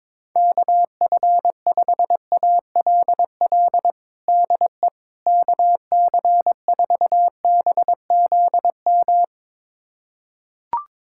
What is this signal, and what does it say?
Morse code, heard as a keyed tone, about 22 wpm, 700 Hz. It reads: KF5ALL DE KC4BZM
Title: USA Callsign DE Callsign - 6 Characters - 22wpm